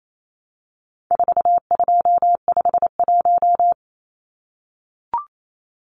Morse code, heard as a keyed tone, about 28 wpm, 700 Hz. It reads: 4251
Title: Numbers - 4-Digit 28wpm